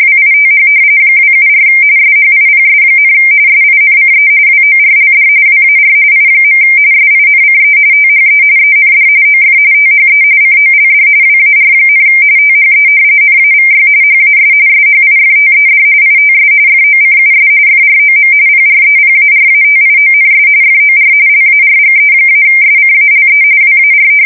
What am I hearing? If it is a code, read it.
RTTY, about 45.45 baud. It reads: WITH 274 SHIPS, SINCE WORLD WAR I. SINCE THE END OF THE COLD WAR, THE NAVY HAS SHIFTED ITS FOCUS FROM PREPARATIONS FOR LARGE-SCALE WAR
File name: fsk170 / usnavy5